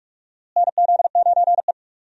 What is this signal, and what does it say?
Morse code, heard as a keyed tone, about 45 wpm, 700 Hz. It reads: NZ9E
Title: USA Call Signs - 4 Characters - Repeated 3x - 45wpm